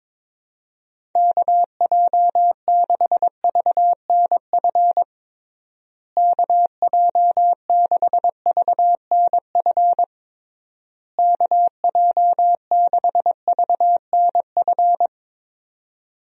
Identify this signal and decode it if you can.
Morse code, heard as a keyed tone, about 21 wpm, 700 Hz. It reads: KJ6VNF KJ6VNF KJ6VNF
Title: USA Call Signs - 6 Characters - Repeated 3x - 22wpm